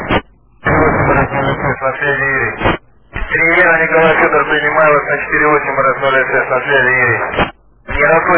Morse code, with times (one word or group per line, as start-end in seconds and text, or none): none